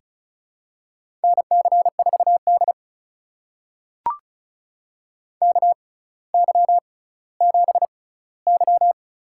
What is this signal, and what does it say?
Morse code, heard as a keyed tone, about 36 wpm, spaced as for 18 wpm, 700 Hz. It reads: NC4D KY7Y
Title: USA Call Signs - 4 Characters - 6x Character Spacing - 35wpm